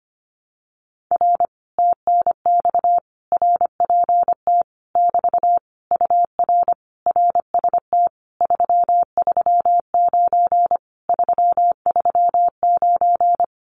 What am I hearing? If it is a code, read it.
Morse code, heard as a keyed tone, about 25 wpm, 700 Hz. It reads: R TNX RPT <BT> UR RST 339 339
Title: Common QSO Phrases Encore 25wpm